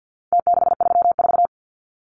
Morse code, heard as a keyed tone, about 50 wpm, 700 Hz. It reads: T634
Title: Numbers - Contest Style - 4-Digit 50wpm